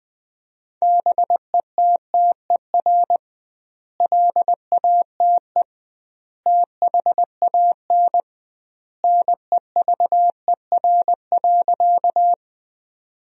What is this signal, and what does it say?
Morse code, heard as a keyed tone, about 21 wpm, 700 Hz. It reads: BETTER LATE THAN NEVER.
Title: 2x Word-Spacing - Sentences from Top 400 Words - 20wpm